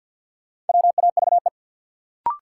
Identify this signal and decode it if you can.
Morse code, heard as a keyed tone, about 47 wpm, 700 Hz. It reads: WAVE
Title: Sentences from Top 1400 Words - New Word Drill - 50wpm